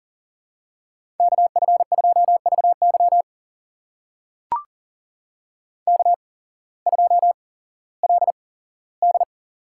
Morse code, heard as a keyed tone, about 40 wpm, spaced as for 19 wpm, 700 Hz. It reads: KF2VY K2LD
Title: USA Call Signs - 8x Character Spacing - 40wpm